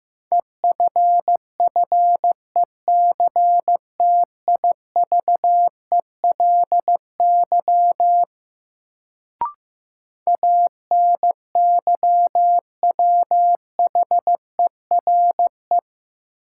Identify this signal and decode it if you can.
Morse code, heard as a keyed tone, about 15 wpm, 700 Hz. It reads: EFFECTIVELY ANYWHERE
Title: Sentences from Top 1400 Words - New Word Drill - 15wpm